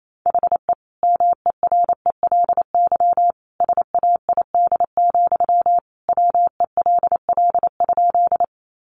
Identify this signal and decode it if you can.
Morse code, heard as a keyed tone, about 28 wpm, 700 Hz. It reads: HE MERELY SAID, WELL?
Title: Sentences from Top 1400 Words 28wpm